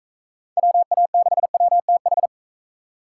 Morse code, heard as a keyed tone, about 40 wpm, 700 Hz. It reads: WA6WTH